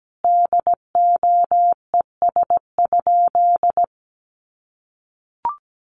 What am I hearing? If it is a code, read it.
Morse code, heard as a keyed tone, about 17 wpm, 700 Hz. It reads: DOES?
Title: Sentences from Top 1300 Words 17wpm